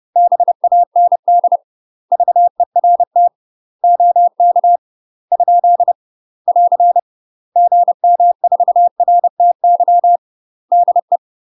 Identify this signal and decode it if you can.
Morse code, heard as a keyed tone, about 30 wpm, 700 Hz. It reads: BAND VERT OK ? <AR> GM4RTY DE
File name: UK MASTER QSOs TRAINING / 30WPM / UK MASTER QSOs 0046